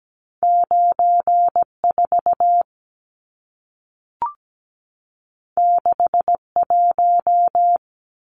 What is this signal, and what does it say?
Morse code, heard as a keyed tone, about 17 wpm, 700 Hz. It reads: 94 61
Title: Numbers - 2-Digit 17wpm